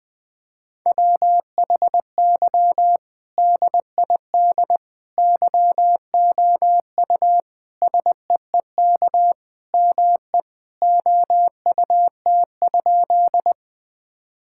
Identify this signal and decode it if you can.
Morse code, heard as a keyed tone, about 20 wpm, 700 Hz. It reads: WHY DID YOU SEEK ME OUT?
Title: Sentences from Top 1600 Words 20wpm